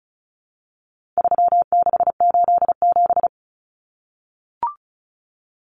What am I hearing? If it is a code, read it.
Morse code, heard as a keyed tone, about 34 wpm, 700 Hz. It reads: 3687
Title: Numbers - 4-Digit 35wpm